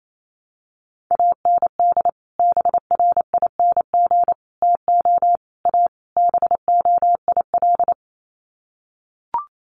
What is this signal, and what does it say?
Morse code, heard as a keyed tone, about 28 wpm, 700 Hz. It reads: AND BRING TO A BOIL
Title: Sets of 5 Words Encore 28wpm